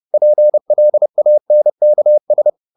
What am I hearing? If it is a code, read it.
Morse code, heard as a keyed tone, about 30 wpm, 600 Hz. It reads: PLANKS
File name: the old man and the sea / the old man and the sea 1 30wpm